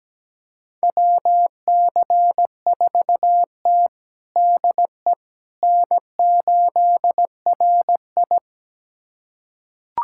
Morse code, heard as a keyed tone, about 17 wpm, 700 Hz. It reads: WC4T DE N8RI
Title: USA Callsign DE Callsign - 4 Characters - 17wpm